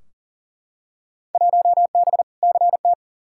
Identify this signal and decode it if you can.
Morse code, heard as a keyed tone, about 40 wpm, 700 Hz. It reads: 1B CT
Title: Field Day Exchanges Encore - 40wpm